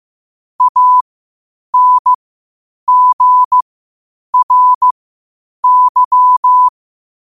Morse code, heard as a keyed tone, about 15 wpm, spaced as for 8 wpm, 1000 Hz. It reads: ANGRY
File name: qotdfiles / 8wpm